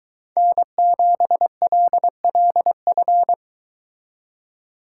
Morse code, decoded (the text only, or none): N7LLF